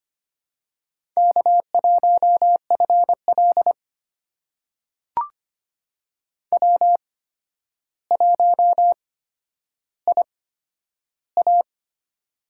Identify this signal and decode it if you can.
Morse code, heard as a keyed tone, about 25 wpm, spaced as for 11 wpm, 700 Hz. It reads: K1FL W1IA
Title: USA Call Signs - 4 Characters - 8x Character Spacing - 25wpm